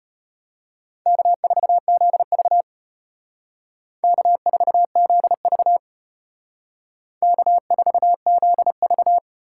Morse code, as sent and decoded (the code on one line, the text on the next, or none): -.- ....- --.. ...- / -.- ....- --.. ...- / -.- ....- --.. ...-
K4ZV K4ZV K4ZV